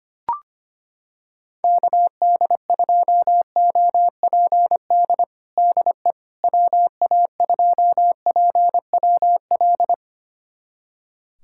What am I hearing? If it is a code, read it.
Morse code, heard as a keyed tone, about 25 wpm, 700 Hz. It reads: KD2OPD DE WA2PWL
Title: USA Callsign DE Callsign 25wpm